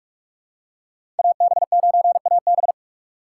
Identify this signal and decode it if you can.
Morse code, heard as a keyed tone, about 44 wpm, 700 Hz. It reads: AB9AB
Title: USA Call Signs - 5 Characters - 6x Character Spacing - 45wpm